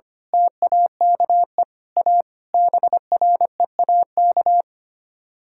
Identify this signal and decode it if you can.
Morse code, heard as a keyed tone, about 24 wpm, 700 Hz. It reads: TAKE A BREAK
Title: Speed-Racing - Sets of 5 Words 25wpm